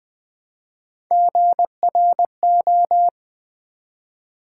Morse code, heard as a keyed tone, about 20 wpm, 700 Hz. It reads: GRO
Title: Build A Word - 1 Letter At A Time - 3 to 5-Letter Words - 20wpm